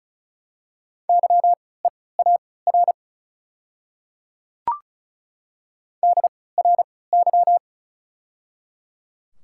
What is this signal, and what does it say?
Morse code, heard as a keyed tone, about 36 wpm, spaced as for 20 wpm, 700 Hz. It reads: YEAR DRY
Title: 3x Character Spacing - Top 500 Words - 35wpm